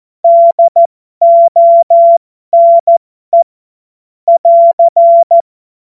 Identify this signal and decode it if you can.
Morse code, heard as a keyed tone, about 14 wpm, spaced as for 12 wpm, 680 Hz. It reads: DONE <AR>
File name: Morsekurs / Witze